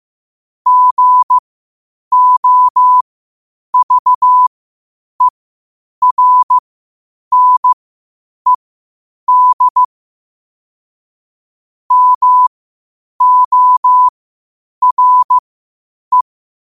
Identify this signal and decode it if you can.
Morse code, heard as a keyed tone, about 15 wpm, spaced as for 8 wpm, 1000 Hz. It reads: GOVERNED MORE
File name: qotdfiles / 8wpm